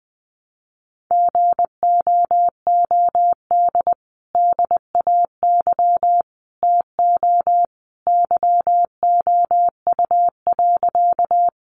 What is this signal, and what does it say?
Morse code, heard as a keyed tone, about 20 wpm, 700 Hz. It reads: GOOD DAY TO YOU.
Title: Sentences from Top 200 Words 20wpm